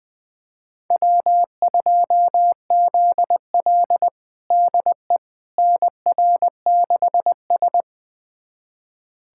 Morse code, heard as a keyed tone, about 20 wpm, 700 Hz. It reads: W2ZL DE NR6S
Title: USA Callsign DE Callsign - 4 Characters - 20wpm